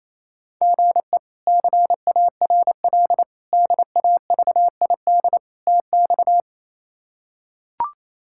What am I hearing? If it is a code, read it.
Morse code, heard as a keyed tone, about 28 wpm, 700 Hz. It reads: GE CARL DAVID TX
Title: K1USN SST QSO Practice - 28wpm